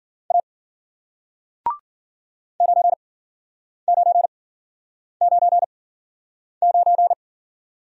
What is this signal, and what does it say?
Morse code, heard as a keyed tone, about 50 wpm, spaced as for 9 wpm, 700 Hz. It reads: A 9999
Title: Speed-Racing Single Letter-Number 40wpm